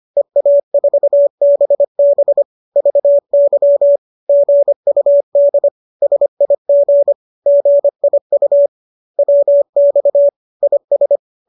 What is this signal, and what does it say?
Morse code, heard as a keyed tone, about 25 wpm, 570 Hz. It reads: EA4BB VY GUD SIG GIU WX IS